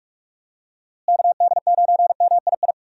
Morse code, heard as a keyed tone, about 46 wpm, 700 Hz. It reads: KD9MII
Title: USA Call Signs - 6 Characters - Repeated 2x - 45wpm